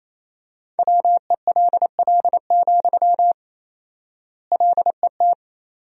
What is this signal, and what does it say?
Morse code, heard as a keyed tone, about 28 wpm, 700 Hz. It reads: WELL, LET
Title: 4x Word-Spacing - Sentences from Top 300 Words - 28wpm